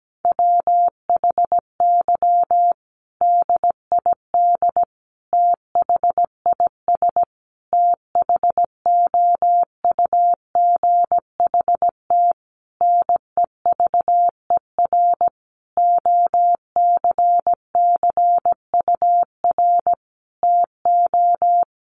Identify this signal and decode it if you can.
Morse code, heard as a keyed tone, about 17 wpm, 700 Hz. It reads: WHY DID THIS THOUGHT NEVER OCCUR TO
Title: Sentences from Top 900 Words 17wpm